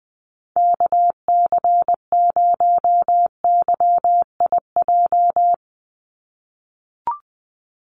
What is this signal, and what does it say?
Morse code, heard as a keyed tone, about 20 wpm, 700 Hz. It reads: KC0YIJ
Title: USA Call Signs 20wpm